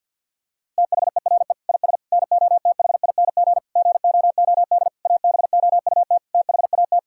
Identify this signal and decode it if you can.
Morse code, heard as a keyed tone, about 50 wpm, 700 Hz. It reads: THERE IS NOTHING GOOD ABOUT THAT
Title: Sentences from Top 1000 Words - Review Part 5 - 50wpm